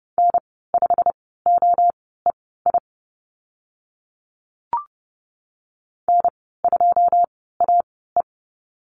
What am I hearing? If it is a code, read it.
Morse code, heard as a keyed tone, about 29 wpm, spaced as for 17 wpm, 700 Hz. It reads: N5OEI N2AE
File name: Lesson 15g - Number 5 - Call Signs 30wpm